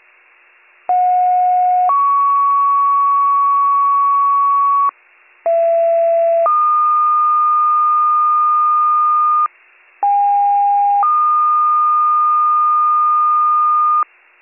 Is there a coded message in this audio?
Yes: two-tone page from 0.9 to 14.0 s, heard as a one-second tone, then a three-second tone, about 730 then 1120 Hz, about 670 then 1220 Hz, about 810 then 1220 Hz.